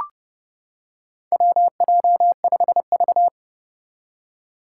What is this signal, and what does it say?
Morse code, heard as a keyed tone, about 28 wpm, 700 Hz. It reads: WJ5V
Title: Four Challenging Letter-Number Combinations 30wpm